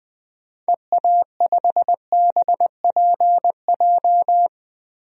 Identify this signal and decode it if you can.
Morse code, heard as a keyed tone, about 20 wpm, 700 Hz. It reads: EA5BPJ
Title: DXCC-100 Call Signs - 6 Characters - Repeated 3x - 20wpm